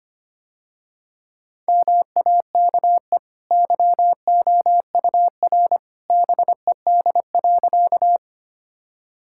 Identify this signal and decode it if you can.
Morse code, heard as a keyed tone, about 25 wpm, 700 Hz. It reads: MAKE YOUR BED.
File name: Easy Sentences with Hints - 25wpm